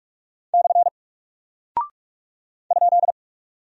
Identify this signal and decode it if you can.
Morse code, heard as a keyed tone, about 44 wpm, 700 Hz. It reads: / ?
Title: Single Character (Letter:Number:Punctuation:Prosign) Alt 45wpm